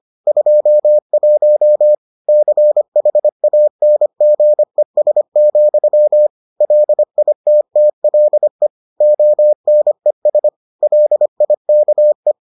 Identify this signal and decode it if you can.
Morse code, heard as a keyed tone, about 25 wpm, 600 Hz. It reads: 21 CHANGES, LITTLE ONES LIKE